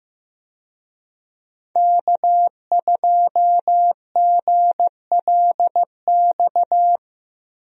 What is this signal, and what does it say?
Morse code, heard as a keyed tone, about 15 wpm, 700 Hz. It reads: K2GLX
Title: USA Call Signs - 5 Characters - 15wpm